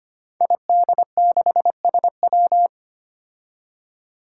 Morse code, heard as a keyed tone, about 25 wpm, 700 Hz. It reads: ID6SW
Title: Five Challenging Letter-Number Combinations 25wpm